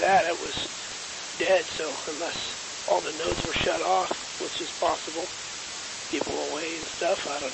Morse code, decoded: none

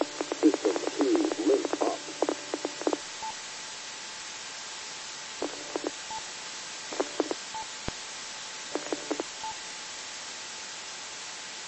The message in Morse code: none

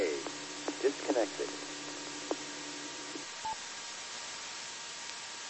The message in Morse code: none

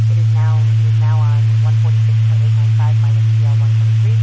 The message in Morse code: none

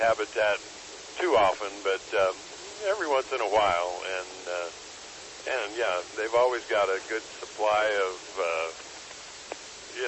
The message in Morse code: none

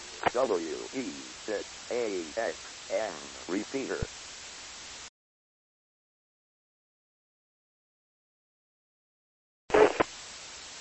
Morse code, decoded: none

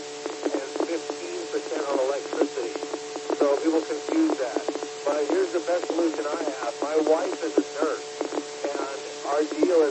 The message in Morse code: none

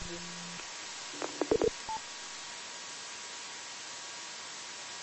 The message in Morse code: none

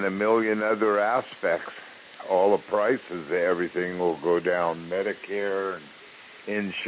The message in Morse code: none

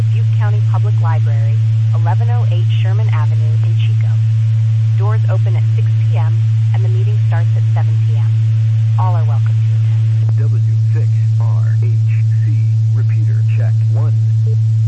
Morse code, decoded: none

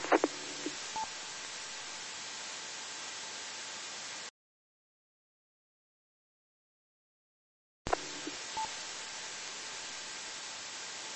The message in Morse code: none

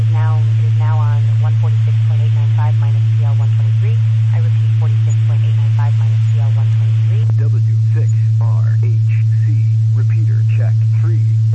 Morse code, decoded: none

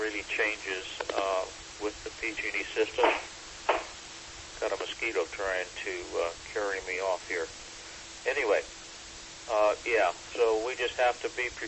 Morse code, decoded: none

